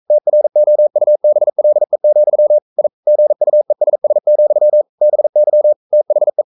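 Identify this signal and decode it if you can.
Morse code, heard as a keyed tone, about 42 wpm, 600 Hz. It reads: TROUBLE, I GUESS, BY THE
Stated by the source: Morse code sending of an English novel at 42 wpm.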